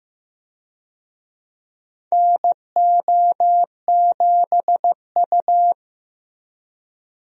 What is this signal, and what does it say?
Morse code, heard as a keyed tone, about 15 wpm, 700 Hz. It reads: NO7U
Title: USA Call Signs - 4 Characters - Repeated 2x - 15wpm